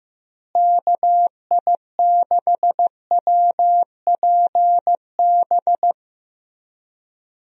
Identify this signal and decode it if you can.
Morse code, heard as a keyed tone, about 15 wpm, 700 Hz. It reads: KI6WPB
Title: USA Call Signs - 6 Characters - Repeated 3x - 15wpm